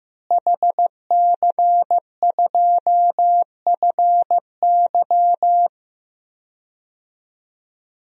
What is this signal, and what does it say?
Morse code, heard as a keyed tone, about 15 wpm, 700 Hz. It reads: HC2FY
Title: DXCC-100 Call Signs - 5 Characters - Repeated 2x - 15wpm